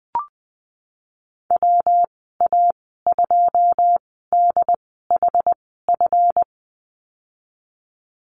Morse code, heard as a keyed tone, about 20 wpm, spaced as for 15 wpm, 700 Hz. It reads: WA2DHF
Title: 179 Lesson 29g - Slash - Call Signs 20wpm